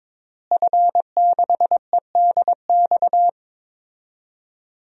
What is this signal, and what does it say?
Morse code, heard as a keyed tone, about 22 wpm, 700 Hz. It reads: F6EDX
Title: DXCC-100 Call Signs - 5 Characters - Repeated 3x - 22wpm